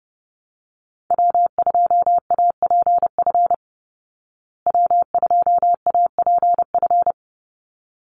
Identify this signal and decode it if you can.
Morse code, heard as a keyed tone, about 30 wpm, 700 Hz. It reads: W2APF W2APF
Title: Call Signs Repeated 3x - 30wpm